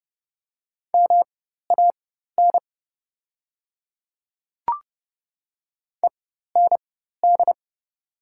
Morse code, heard as a keyed tone, about 29 wpm, spaced as for 14 wpm, 700 Hz. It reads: MAN END
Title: 4x Character Spacing - Top 300 Words - 30wpm